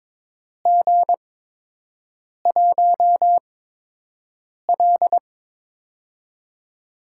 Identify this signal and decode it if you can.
Morse code, heard as a keyed tone, about 22 wpm, spaced as for 6 wpm, 700 Hz. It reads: G1L
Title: USA Call Signs - 4 Characters - 8x Character Spacing - 22wpm